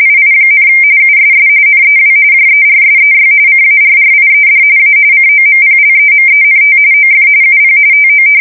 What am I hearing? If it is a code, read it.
RTTY, about 45.45 baud. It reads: II, USS NEW JERSEY CARRIED A COMPLEMENT OF UP TO
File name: fsk170 / battleshipnj